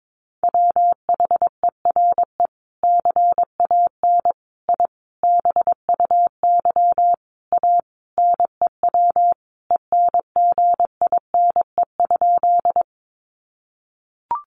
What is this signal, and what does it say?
Morse code, heard as a keyed tone, about 22 wpm, 700 Hz. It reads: WHERE CAN I BUY A NEW ENGINE?
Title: Sentences from Top 1500 Words 22wpm